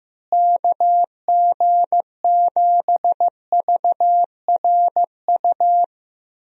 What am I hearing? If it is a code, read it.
Morse code, heard as a keyed tone, about 16 wpm, 700 Hz. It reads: KG7VRU
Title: USA Call Signs - 6 Characters - Repeated 3x - 15wpm